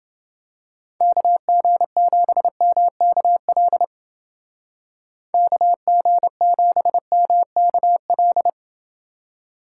Morse code, heard as a keyed tone, about 28 wpm, 700 Hz. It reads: KG7MKL KG7MKL